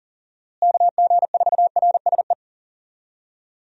Morse code, heard as a keyed tone, about 40 wpm, 700 Hz. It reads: KG4RSE
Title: USA Call Signs - 6 Characters - Repeated 2x - 40wpm